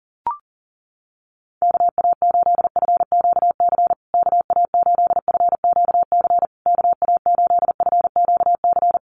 Morse code, heard as a keyed tone, about 40 wpm, 700 Hz. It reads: KA8FQC KA8FQC KA8FQC
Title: USA Call Signs Repeated 3x 40wpm